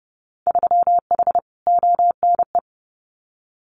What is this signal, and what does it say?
Morse code, heard as a keyed tone, about 30 wpm, 700 Hz. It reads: 3H ONE
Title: Winter Field Day Exchanges Encore 30wpm